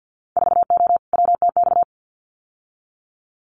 Morse code, heard as a keyed tone, about 50 wpm, 700 Hz. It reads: 4O WTX
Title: Winter Field Day Exchanges Encore 50wpm